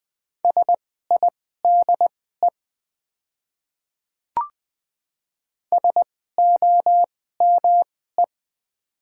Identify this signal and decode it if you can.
Morse code, heard as a keyed tone, about 20 wpm, spaced as for 14 wpm, 690 Hz. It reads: SIDE SOME
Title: 2x Character Spacing - Top 100 Words - 20wpm